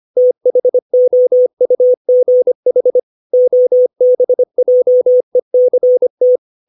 Morse code, heard as a keyed tone, about 25 wpm, 500 Hz. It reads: THOUGH OBJECT
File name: 632 words / 632 Words 25WPM